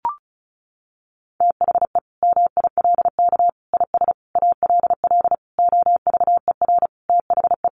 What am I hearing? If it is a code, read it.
Morse code, heard as a keyed tone, about 34 wpm, 700 Hz. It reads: THE MILK IS ALL OVER THE
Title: Additional Easy Phrases 35wpm